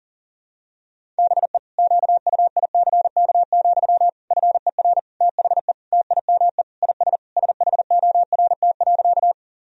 Morse code, heard as a keyed tone, about 40 wpm, 700 Hz. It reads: BE QUICK, FER THE TIME IS SHORT.
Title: Sentences from Top 500 Words and Abbreviations 40wpm